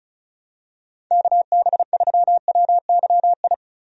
Morse code, heard as a keyed tone, about 34 wpm, 700 Hz. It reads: KB3WYI
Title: USA Call Signs - 6 Characters - Repeated 3x - 35wpm